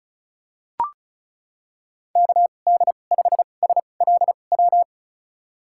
Morse code, heard as a keyed tone, about 36 wpm, spaced as for 25 wpm, 700 Hz. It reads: KD5SLW
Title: USA Call Signs - 6 Characters - 2x Character Spacing - 35wpm